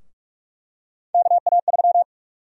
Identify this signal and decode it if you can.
Morse code, heard as a keyed tone, about 42 wpm, 700 Hz. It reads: KA3
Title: USA Call Sign Prefixes - 45wpm